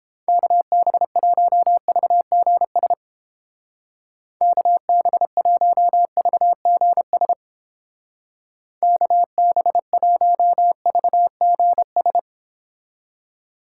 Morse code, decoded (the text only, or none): KB1VGS KB1VGS KB1VGS